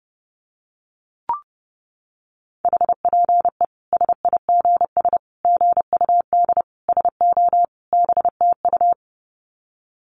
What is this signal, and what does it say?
Morse code, heard as a keyed tone, about 30 wpm, 700 Hz. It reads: HPE SIGS GUD SO BTU
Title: Common QSO Phrases 30wpm